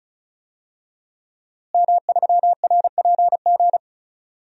Morse code, heard as a keyed tone, about 36 wpm, 700 Hz. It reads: M3RPG
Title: DXCC-100 Call Signs - 5 Characters - Repeated 2x - 35wpm